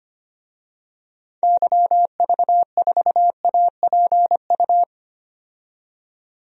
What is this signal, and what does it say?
Morse code, heard as a keyed tone, about 25 wpm, 700 Hz. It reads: YV4APU
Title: DXCC-100 Call Signs - 6 Characters - 25wpm